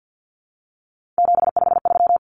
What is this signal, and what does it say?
Morse code, heard as a keyed tone, about 50 wpm, 700 Hz. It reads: 753
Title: Common Cut Numbers - 3-Digit 50wpm